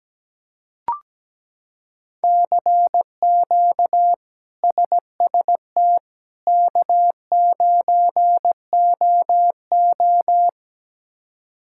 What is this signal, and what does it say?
Morse code, heard as a keyed tone, about 17 wpm, 700 Hz. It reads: CQ SST K9OO
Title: K1USN SST QSO Practice - 17wpm